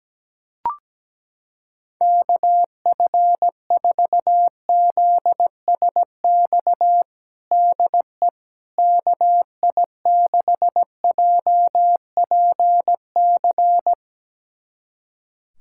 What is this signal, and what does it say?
Morse code, heard as a keyed tone, about 17 wpm, 700 Hz. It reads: KF4ZSX DE KI6JPC
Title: USA Callsign DE Callsign 17wpm